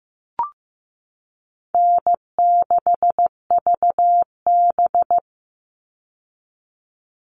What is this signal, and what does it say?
Morse code, heard as a keyed tone, about 15 wpm, 700 Hz. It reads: N6VB
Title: All Cut Numbers - 4-Digit 15wpm